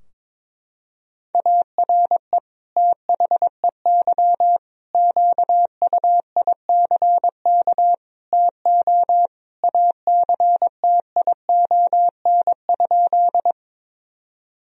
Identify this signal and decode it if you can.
Morse code, heard as a keyed tone, about 21 wpm, 700 Hz. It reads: ARE THEY QUICK TO ACTION?
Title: Sentences from Top 1000 Words - Review Part 4 - 22wpm